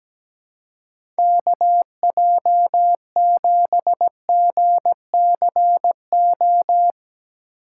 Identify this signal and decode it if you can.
Morse code, heard as a keyed tone, about 17 wpm, 700 Hz. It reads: KJ7GCO
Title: USA Call Signs - 3x Character Spacing - 17wpm